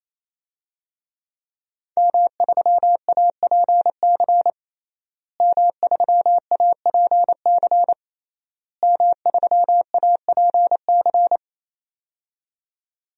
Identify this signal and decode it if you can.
Morse code, heard as a keyed tone, about 28 wpm, 700 Hz. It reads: M3APC M3APC M3APC
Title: DXCC-100 Call Signs - 5 Characters - Repeated 3x - 28wpm